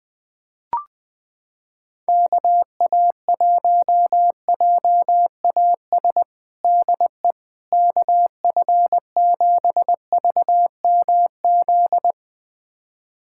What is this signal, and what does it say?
Morse code, heard as a keyed tone, about 20 wpm, 700 Hz. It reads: KA1JAS DE KF7VMZ
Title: USA Callsign DE Callsign - 6 Characters - 20wpm